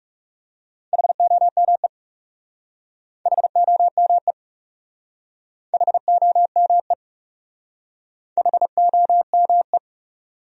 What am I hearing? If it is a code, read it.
Morse code, heard as a keyed tone, about 40 wpm, 700 Hz. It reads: HOME HOME HOME HOME